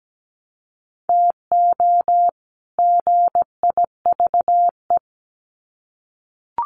Morse code, heard as a keyed tone, about 17 wpm, 700 Hz. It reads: TO GIVE
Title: Sets of 2 Words 17wpm